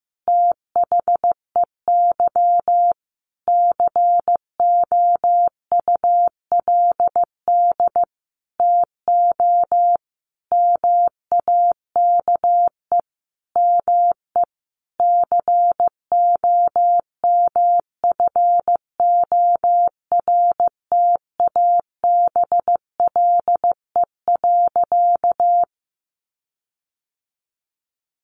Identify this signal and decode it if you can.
Morse code, heard as a keyed tone, about 15 wpm, 700 Hz. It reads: THEY COULD TO MAKE ME COMFORTABLE.
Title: Sentences from Top 1500 Words 15wpm